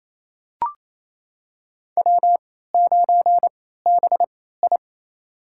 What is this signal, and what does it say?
Morse code, heard as a keyed tone, about 28 wpm, spaced as for 16 wpm, 700 Hz. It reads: W9BI
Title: USA Call Signs - 4 Characters - 3x Character Spacing - 28wpm